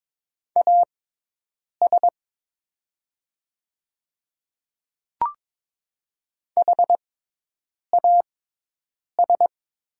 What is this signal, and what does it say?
Morse code, heard as a keyed tone, about 22 wpm, spaced as for 8 wpm, 700 Hz. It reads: AS HAS
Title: 6x Character Spacing - Top 200 Words - 22wpm